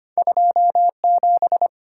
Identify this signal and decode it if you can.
Morse code, heard as a keyed tone, about 33 wpm, 690 Hz. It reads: 27